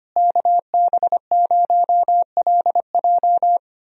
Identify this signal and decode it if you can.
Morse code, heard as a keyed tone, about 25 wpm, 700 Hz. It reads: KB0LJ
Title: USA Call Signs - 5 Characters - Repeated 2x - 25wpm